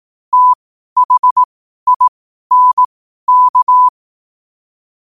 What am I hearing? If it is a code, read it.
Morse code, heard as a keyed tone, about 18 wpm, spaced as for 12 wpm, 1000 Hz. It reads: THINK
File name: qotdfiles / 12wpm